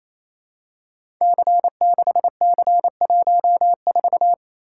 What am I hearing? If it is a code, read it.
Morse code, heard as a keyed tone, about 27 wpm, 700 Hz. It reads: C6C14
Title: Code Groups - 5 Letter:Numbers - 28wpm